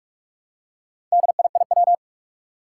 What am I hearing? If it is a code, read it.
Morse code, heard as a keyed tone, about 44 wpm, 700 Hz. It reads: DIIW
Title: Four Challenging Letter-Number Combinations 45wpm